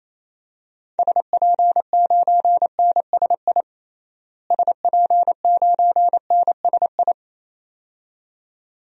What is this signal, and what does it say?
Morse code, heard as a keyed tone, about 28 wpm, 700 Hz. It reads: SP9NSI SP9NSI
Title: DXCC-100 Call Signs - 6 Characters - Repeated 2x - 28wpm